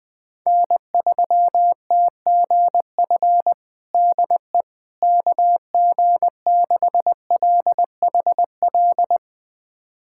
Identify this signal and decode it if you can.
Morse code, heard as a keyed tone, about 20 wpm, 700 Hz. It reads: N3TGF DE KG6LHL
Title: USA Callsign DE Callsign 20wpm